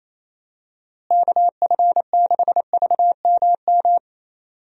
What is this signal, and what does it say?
Morse code, heard as a keyed tone, about 28 wpm, 700 Hz. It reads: KF6VMM
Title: USA Call Signs - 6 Characters - Repeated 3x - 28wpm